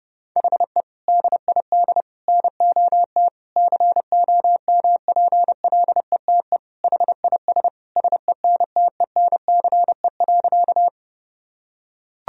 Morse code, heard as a keyed tone, about 32 wpm, 700 Hz. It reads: HE DID NOT COMPLETE HIS SENTENCE.
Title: Sentences from Top 1000 Words - Review Part 1 - 30wpm